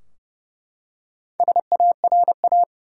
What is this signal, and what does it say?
Morse code, heard as a keyed tone, about 32 wpm, 700 Hz. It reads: SARA
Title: USA Common Names - Max 4 Letters - 30wpm